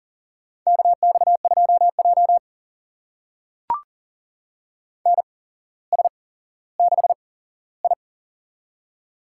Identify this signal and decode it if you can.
Morse code, heard as a keyed tone, about 40 wpm, spaced as for 17 wpm, 700 Hz. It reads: KX2J NS6I
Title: USA Call Signs - 4 Characters - 8x Character Spacing - 40wpm